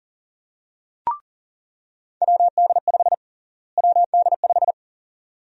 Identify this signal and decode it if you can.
Morse code, heard as a keyed tone, about 40 wpm, 700 Hz. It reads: WD5 WD5
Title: USA Call Sign Prefixes - Repeated 2x - 40wpm